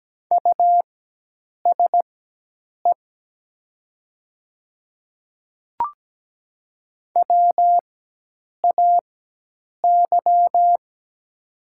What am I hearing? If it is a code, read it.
Morse code, heard as a keyed tone, about 17 wpm, spaced as for 8 wpm, 700 Hz. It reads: USE WAY
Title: 4x Character Spacing - Top 200 Words - 17wpm